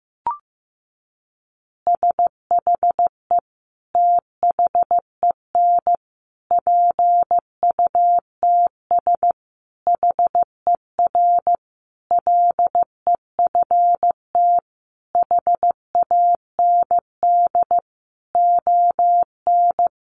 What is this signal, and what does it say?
Morse code, heard as a keyed tone, about 15 wpm, 700 Hz. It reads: SHE THEN PUTS HER LEFT HAND ON
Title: Easy Phrases 15wpm